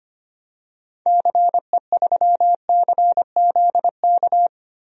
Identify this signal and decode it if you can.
Morse code, heard as a keyed tone, about 25 wpm, 700 Hz. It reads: CE3CZK
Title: DXCC-100 Call Signs - 6 Characters - 25wpm